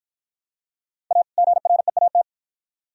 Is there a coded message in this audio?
Yes: Morse code from 1.1 to 2.2 s, heard as a keyed tone, about 53 wpm, 700 Hz. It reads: A GREAT